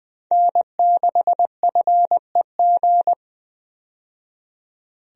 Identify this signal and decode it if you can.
Morse code, heard as a keyed tone, about 20 wpm, 700 Hz. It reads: N6FEG
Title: USA Call Signs - 5 Characters - 20wpm